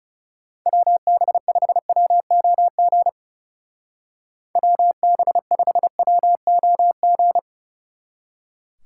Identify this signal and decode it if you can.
Morse code, heard as a keyed tone, about 37 wpm, 700 Hz. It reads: WB5WOG WB5WOG
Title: Speed Racing - USA 2x3 Call Signs 30wpm